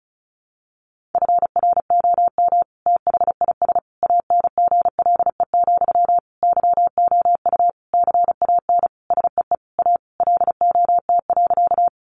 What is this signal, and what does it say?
Morse code, heard as a keyed tone, about 34 wpm, 700 Hz. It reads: FROM THIS ANGLE, YOU CAN SEE A LOT.
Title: Sentences from Top 1500 Words 35wpm